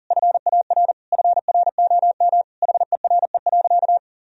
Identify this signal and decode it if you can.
Morse code, heard as a keyed tone, about 40 wpm, 700 Hz. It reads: FAR FROM HERE.
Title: Sentences from Top 1000 Words - Review Part 4 - 40wpm